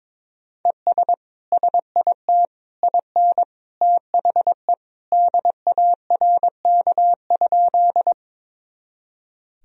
Morse code, heard as a keyed tone, about 22 wpm, 700 Hz. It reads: ES SIT IN THE DARK?
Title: Sentences from Top 500 Words and Abbreviations 22wpm